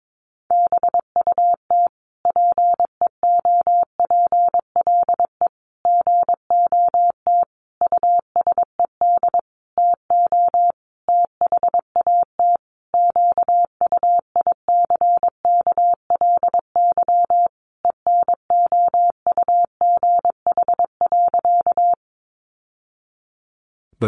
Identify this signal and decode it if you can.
Morse code, heard as a keyed tone, about 22 wpm, 700 Hz. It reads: BUT PEOPLE GOT USED TO THAT QUICKLY ENOUGH.
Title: Sentences from Top 700 Words 22wpm